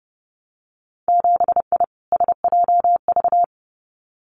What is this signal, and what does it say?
Morse code, heard as a keyed tone, about 30 wpm, 700 Hz. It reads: 7I SJV
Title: Winter Field Day Exchanges Encore 30wpm